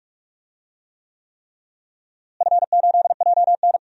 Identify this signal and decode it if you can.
Morse code, heard as a keyed tone, about 46 wpm, 700 Hz. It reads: F8JN
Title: DXCC-100 Call Signs - 4 Characters - Repeated 3x - 45wpm